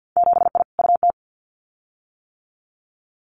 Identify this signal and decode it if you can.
Morse code, heard as a keyed tone, about 50 wpm, 700 Hz. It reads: 7I UT
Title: Winter Field Day Exchanges Encore 50wpm